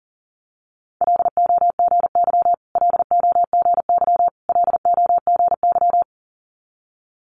Morse code, heard as a keyed tone, about 38 wpm, 700 Hz. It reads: LOGY LOGY LOGY
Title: Multi-letter Phonemes Repeated 3x 40wpm